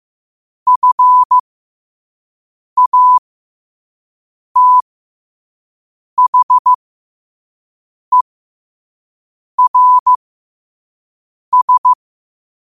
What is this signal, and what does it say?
Morse code, heard as a keyed tone, about 15 wpm, spaced as for 5 wpm, 1000 Hz. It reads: FATHERS
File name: qotdfiles / 5wpm